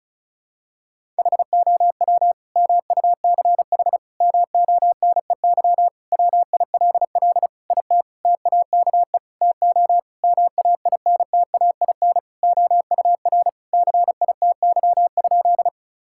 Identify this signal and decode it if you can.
Morse code, heard as a keyed tone, about 34 wpm, 700 Hz. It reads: HOW MUCH MONEY WILL IT TAKE TO MAINTAIN OUR CITY?
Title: Sentences from Top 1000 Words - Review Part 5 - 35wpm